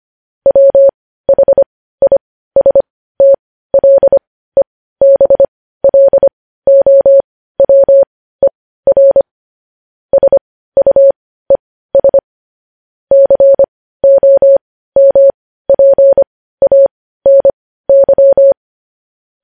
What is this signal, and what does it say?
Morse code, heard as a keyed tone, about 25 wpm, spaced as for 15 wpm, 550 Hz. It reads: WHISTLEBLOWER SUES COMPANY
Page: news headlines in morse code at 15 wpm